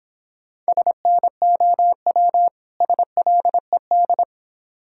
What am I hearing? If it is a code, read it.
Morse code, heard as a keyed tone, about 26 wpm, 700 Hz. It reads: SNOW SLED